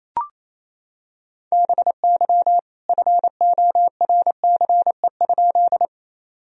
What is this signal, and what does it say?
Morse code, heard as a keyed tone, about 28 wpm, 700 Hz. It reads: BY FORCE?
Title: Sentences from Top 1000 Words - Review Part 3 - 28wpm